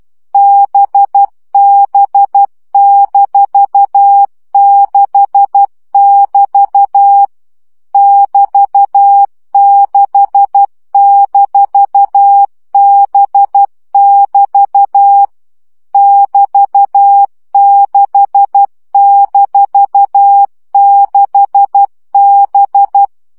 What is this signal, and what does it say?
Morse code, heard as a keyed tone, about 12 wpm, 800 Hz. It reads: BB-6<BT> <BT>6-B<BT> <BT>6-6B
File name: CW gyakorlas b 6 egyenlo minusz 10 baud